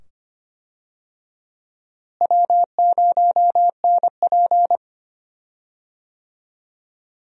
Morse code, heard as a keyed tone, about 25 wpm, 700 Hz. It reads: W0NP
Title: USA Call Signs - 4 Characters - Repeated 2x - 25wpm